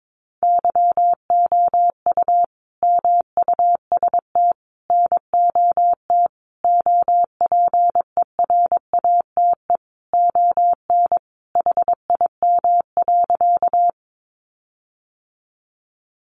Morse code, heard as a keyed tone, about 22 wpm, 700 Hz. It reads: YOU MUST NOT OPERATE ON HIM.
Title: Sentences from Top 1500 Words 22wpm